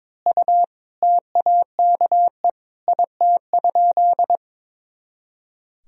Morse code, heard as a keyed tone, about 22 wpm, 700 Hz. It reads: U TAKE IT?
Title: Sentences from Top 200 Words and Abbreviations 22wpm